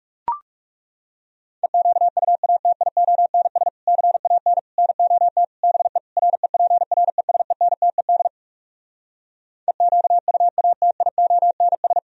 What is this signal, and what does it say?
Morse code, heard as a keyed tone, about 40 wpm, 700 Hz. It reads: EQUATIONS CAN NOT BE REPRESENTED EQUATIONS